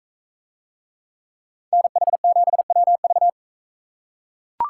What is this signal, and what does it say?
Morse code, heard as a keyed tone, about 42 wpm, 700 Hz. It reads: NH7WV